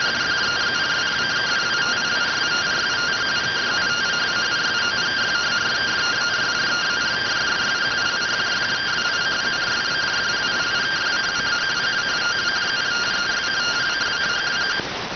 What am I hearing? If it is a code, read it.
Two-tone teleprinter signal, about 170 Hz shift.